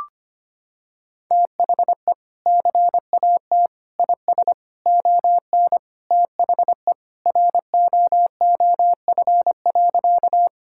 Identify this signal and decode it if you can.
Morse code, heard as a keyed tone, about 25 wpm, 700 Hz. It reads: THE CAT IS ON THE ROOF.
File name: practice / 25wpm / Sentences with 3-Word Phrases - 25wpm